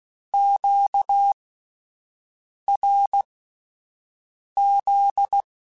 Morse code, heard as a keyed tone, about 16 wpm, spaced as for 5 wpm, 800 Hz. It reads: QRZ
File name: (spaced) / Q-Calls Spaced-mp3s / Q Calls sp Once @10